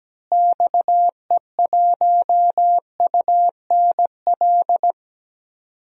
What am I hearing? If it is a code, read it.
Morse code, heard as a keyed tone, about 17 wpm, 700 Hz. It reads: XE1UNL
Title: DXCC-100 Call Signs - 6 Characters - Repeated 3x - 17wpm